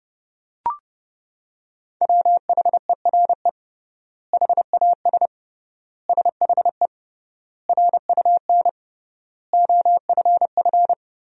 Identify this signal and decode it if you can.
Morse code, heard as a keyed tone, about 30 wpm, 700 Hz. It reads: WHERE HAS SHE RUN OFF
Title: 3x Word-Spacing - Sentences from Top 300 Words - 30wpm